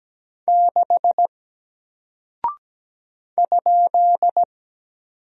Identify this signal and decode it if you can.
Morse code, heard as a keyed tone, about 17 wpm, 700 Hz. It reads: 6 ?
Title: Single Character (Letter:Number:Punctuation:Prosign) 17wpm